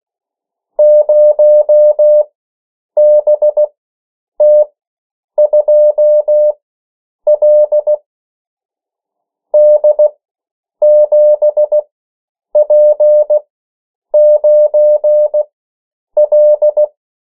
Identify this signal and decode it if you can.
Morse code, heard as a keyed tone, about 16 wpm, spaced as for 8 wpm, 600 Hz. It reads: 0BT2L D7P9L
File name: morse mp3 / 14 urok 13